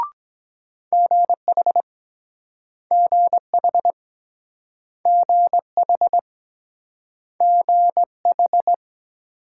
Morse code, heard as a keyed tone, about 24 wpm, 700 Hz. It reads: GH GH GH GH